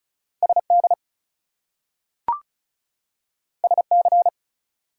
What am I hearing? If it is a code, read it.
Morse code, heard as a keyed tone, about 36 wpm, 700 Hz. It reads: SD SC
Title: US States and Canadian Provinces 35wpm